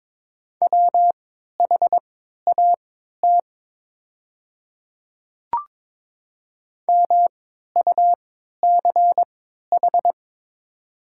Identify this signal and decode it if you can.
Morse code, heard as a keyed tone, about 22 wpm, spaced as for 12 wpm, 700 Hz. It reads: WHAT MUCH
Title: 3x Character Spacing - Top 200 Words - 22wpm